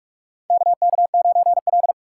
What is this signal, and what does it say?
Morse code, heard as a keyed tone, about 48 wpm, 700 Hz. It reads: KK9L
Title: USA Call Signs - 4 Characters - 2x Character Spacing - 45wpm